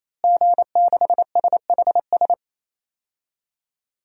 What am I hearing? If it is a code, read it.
Morse code, heard as a keyed tone, about 27 wpm, 700 Hz. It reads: G6SHS
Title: DXCC-100 Call Signs - 5 Characters - Repeated 3x - 28wpm